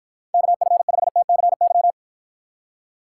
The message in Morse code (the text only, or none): KF5TCY